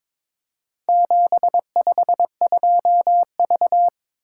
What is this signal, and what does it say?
Morse code, heard as a keyed tone, about 22 wpm, 700 Hz. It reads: 752V
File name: practice / 22wpm / Four Letter-Number Combinations - 22wpm